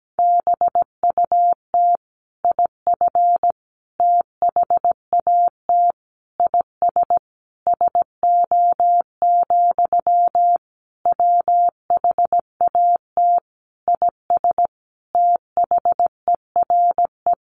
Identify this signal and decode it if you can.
Morse code, heard as a keyed tone, about 17 wpm, 700 Hz. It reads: BUT IF THAT IS SO, WHAT IS THERE
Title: Sentences from Top 200 Words 17wpm